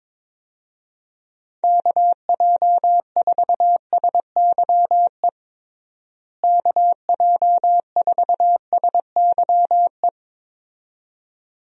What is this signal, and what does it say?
Morse code, heard as a keyed tone, about 22 wpm, 700 Hz. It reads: KJ4SYE KJ4SYE
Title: USA Call Signs - 6 Characters - Repeated 2x - 22wpm